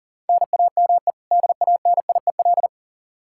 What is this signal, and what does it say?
Morse code, heard as a keyed tone, about 40 wpm, 700 Hz. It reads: NAME DANIEL
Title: Practice QSO - Exchange 1 - Beginning - 40wpm